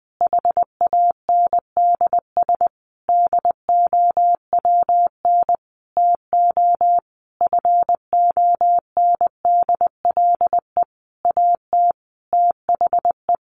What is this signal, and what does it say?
Morse code, heard as a keyed tone, about 20 wpm, 700 Hz. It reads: HANDS DOWN TO FONDLE AT THE
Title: A Sound Of Thunder 20wpm